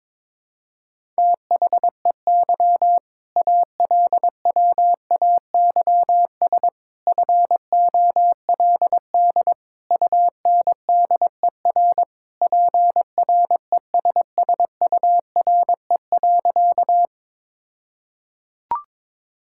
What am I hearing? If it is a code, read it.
Morse code, heard as a keyed tone, about 22 wpm, 700 Hz. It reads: THEY ALWAYS FOLD UNDER PRESSURE.
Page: morse-code-ninja-practice-22wpm